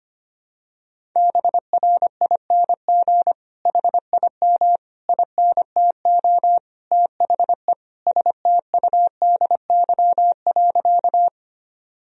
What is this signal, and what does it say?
Morse code, heard as a keyed tone, about 25 wpm, 700 Hz. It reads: BRING HIM INTO THE STUDY.
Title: Sentences from Top 1000 Words - Review Part 1 - 25wpm